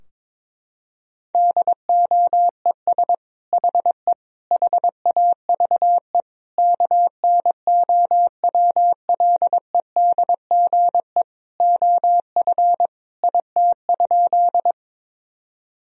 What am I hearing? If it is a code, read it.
Morse code, heard as a keyed tone, about 22 wpm, 700 Hz. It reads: DOES HE HAVE KNOWLEDGE OF IT?
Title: Sentences from Top 1000 Words - Review Part 3 - 22wpm